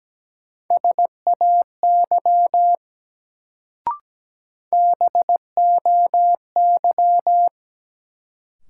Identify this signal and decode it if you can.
Morse code, heard as a keyed tone, about 17 wpm, 700 Hz. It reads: SAY BOY
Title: Sentences from Top 200 Words - New Word Drill - 17wpm